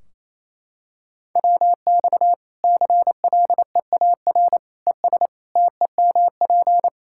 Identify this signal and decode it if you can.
Morse code, heard as a keyed tone, about 28 wpm, 700 Hz. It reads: WX CLEAR ES TEMP
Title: K1USN SST QSO Practice 28wpm